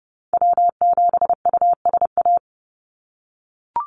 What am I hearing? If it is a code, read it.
Morse code, heard as a keyed tone, about 30 wpm, 700 Hz. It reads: W7USA
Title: NA SOTA Activator Callsigns 30wpm